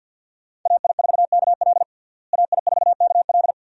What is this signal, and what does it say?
Morse code, heard as a keyed tone, about 50 wpm, 700 Hz. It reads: AI4KL AI4KL
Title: USA Call Signs - 5 Characters - Repeated 2x - 50wpm